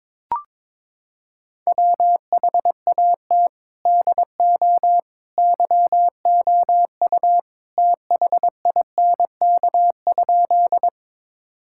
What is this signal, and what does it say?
Morse code, heard as a keyed tone, about 22 wpm, 700 Hz. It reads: WHAT DO YOU THINK?
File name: practice / 22wpm / Sentences with 2-Word Phrases - 22wpm